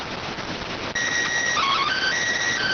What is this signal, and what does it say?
SSTV (slow-scan television), PD 290.